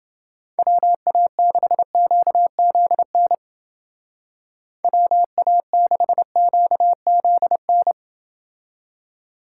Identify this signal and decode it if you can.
Morse code, heard as a keyed tone, about 29 wpm, 700 Hz. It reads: WA6QZN WA6QZN